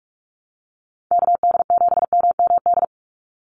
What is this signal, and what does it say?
Morse code, heard as a keyed tone, about 46 wpm, 700 Hz. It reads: KD7MMD
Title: USA Call Signs 45wpm